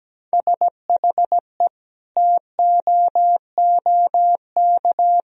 Morse code, heard as a keyed tone, about 17 wpm, 700 Hz. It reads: SHE TOOK
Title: Sentences from Top 1000 Words - Review Part 2 - 17wpm